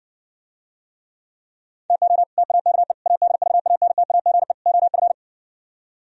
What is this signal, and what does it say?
Morse code, heard as a keyed tone, about 69 wpm, 700 Hz. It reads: TO TAKE ADVANTAGE OF